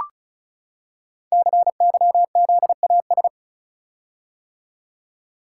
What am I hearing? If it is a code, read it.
Morse code, heard as a keyed tone, about 36 wpm, 700 Hz. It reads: CYZAS
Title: Code Groups - 5 Letters - 35wpm